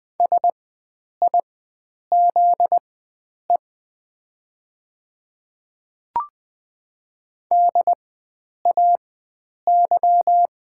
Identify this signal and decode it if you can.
Morse code, heard as a keyed tone, about 20 wpm, spaced as for 9 wpm, 700 Hz. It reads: SIZE DAY
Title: 4x Character Spacing - Top 500 Words - 20wpm